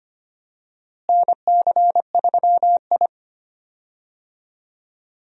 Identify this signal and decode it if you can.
Morse code, heard as a keyed tone, about 25 wpm, 700 Hz. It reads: NC3I
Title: USA Call Signs - 4 Characters - Repeated 2x - 25wpm